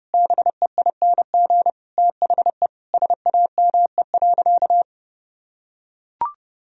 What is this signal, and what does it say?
Morse code, heard as a keyed tone, about 30 wpm, 700 Hz. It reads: BEING THE SAME.
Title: Sentences from Top 1000 Words - Review Part 3 - 30wpm